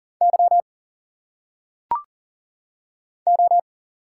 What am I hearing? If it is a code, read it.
Morse code, heard as a keyed tone, about 40 wpm, 700 Hz. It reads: Y O
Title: Single Letters - Phonetic Pronunciation 40wpm